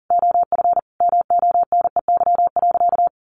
Morse code, heard as a keyed tone, about 40 wpm, 700 Hz. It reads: OF MONEY.
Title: Sentences from Top 1400 Words 40wpm